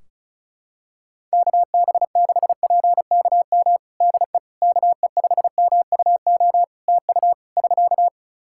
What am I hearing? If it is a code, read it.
Morse code, heard as a keyed tone, about 34 wpm, 700 Hz. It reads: KB6PKM DE KE5MUO TU <SK>
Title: Practice QSO - Exchange 4 - Ending - 35wpm